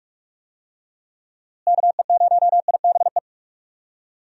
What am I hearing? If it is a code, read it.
Morse code, heard as a keyed tone, about 46 wpm, 700 Hz. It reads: KE0IBE
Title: USA Call Signs - 6 Characters - Repeated 3x - 45wpm